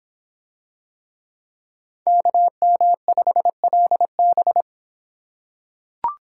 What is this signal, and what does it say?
Morse code, heard as a keyed tone, about 26 wpm, 700 Hz. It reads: KM5LB